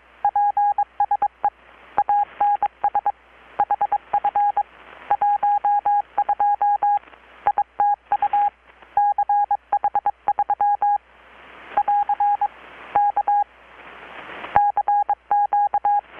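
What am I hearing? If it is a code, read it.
Morse code, heard as a keyed tone, about 22 wpm, 820 Hz. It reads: PSE ANS HF 12 ITU CH3 <AR> K CQ
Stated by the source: Morse from KPH's Night of Nights XX maritime radiotelegraphy event, received in Chiapas, Mexico about 3,800 km away on a Kenwood R-600 with a grounded 10 m long-wire antenna.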